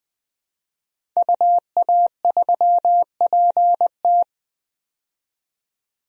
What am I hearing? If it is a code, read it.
Morse code, heard as a keyed tone, about 21 wpm, 700 Hz. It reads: UA3PT
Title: DXCC-100 Call Signs - 5 Characters - 20wpm